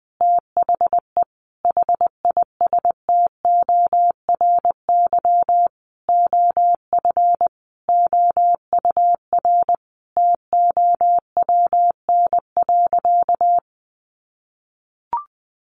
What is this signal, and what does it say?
Morse code, heard as a keyed tone, about 20 wpm, 700 Hz. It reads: THE HISTORY OF OUR TOWN.
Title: Sentences from Top 1500 Words 20wpm